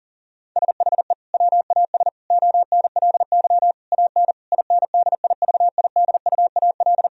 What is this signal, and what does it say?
Morse code, heard as a keyed tone, about 40 wpm, 700 Hz. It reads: SHE WAS ONLY AN INDIVIDUAL
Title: Sentences from Top 1000 Words - Review Part 4 - 40wpm